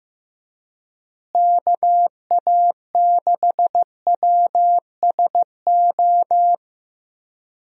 Morse code, heard as a keyed tone, about 15 wpm, 700 Hz. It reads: KA6WSO
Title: USA Call Signs - 6 Characters - Repeated 2x - 15wpm